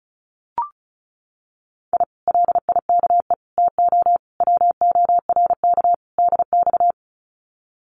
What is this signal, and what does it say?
Morse code, heard as a keyed tone, about 36 wpm, 700 Hz. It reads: I LIKE TO WORK DX
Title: Common QSO Phrases Encore 35wpm